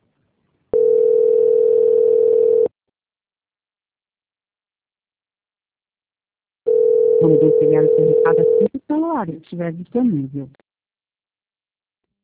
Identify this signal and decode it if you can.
Ringback tone (North America), 440 + 480 Hz, with speech.
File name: out- -2022- - - .59